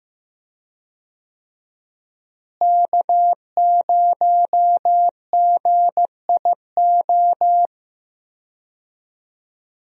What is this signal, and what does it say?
Morse code, heard as a keyed tone, about 15 wpm, 700 Hz. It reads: K0GIO
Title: USA Call Signs - 5 Characters - Repeated 3x - 15wpm